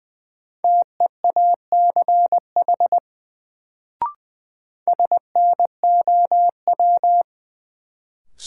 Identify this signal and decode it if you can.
Morse code, heard as a keyed tone, about 20 wpm, 700 Hz. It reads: TEACH SNOW
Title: Sentences from Top 500 Words - New Word Drill - 20wpm